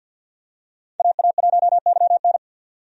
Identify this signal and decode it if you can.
Morse code, heard as a keyed tone, about 52 wpm, 700 Hz. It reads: AA1YN
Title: USA Call Signs - 5 Characters - 50wpm